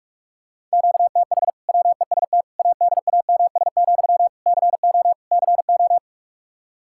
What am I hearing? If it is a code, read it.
Morse code, heard as a keyed tone, about 46 wpm, 700 Hz. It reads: QTH WEST ADAMS, CO CO
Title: Practice QSO - Exchange 1 - Beginning - 45wpm